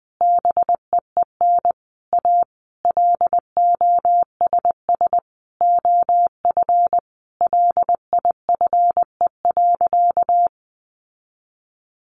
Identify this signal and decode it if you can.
Morse code, heard as a keyed tone, about 19 wpm, 700 Hz. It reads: BEEN A LOSS OF LIFE.
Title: Sentences from Top 1400 Words 20wpm